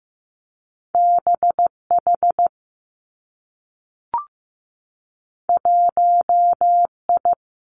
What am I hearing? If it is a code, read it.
Morse code, heard as a keyed tone, about 15 wpm, 700 Hz. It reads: BH 1I
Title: Two Challenging Letter-Number Combinations 15wpm